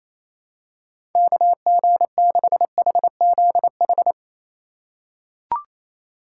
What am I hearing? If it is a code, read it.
Morse code, heard as a keyed tone, about 28 wpm, 700 Hz. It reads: KG6HZH
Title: USA Call Signs - 6 Characters - 3x Character Spacing - 28wpm